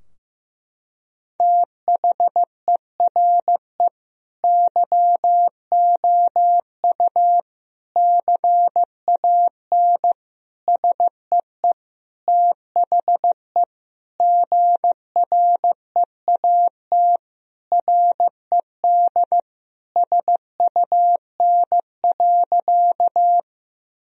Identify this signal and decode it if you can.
Morse code, heard as a keyed tone, about 15 wpm, 700 Hz. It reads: THERE YOU CAN SEE THE GREAT RED SUN.
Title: Sentences from Top 1000 Words - Review Part 2 - 15wpm